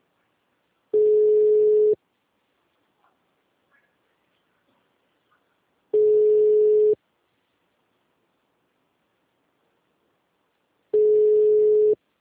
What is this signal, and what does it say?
Ringback tone (Europe, 425 Hz).